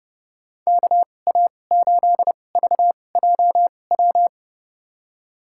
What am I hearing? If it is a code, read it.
Morse code, heard as a keyed tone, about 30 wpm, spaced as for 22 wpm, 700 Hz. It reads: KA8VJW
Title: USA Call Signs - 6 Characters - 2x Character Spacing - 30wpm